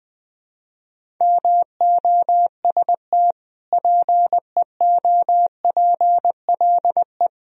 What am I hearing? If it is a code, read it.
Morse code, heard as a keyed tone, about 20 wpm, 700 Hz. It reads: MOST PEOPLE
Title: Sentences from Top 1000 Words - Review Part 1 - 20wpm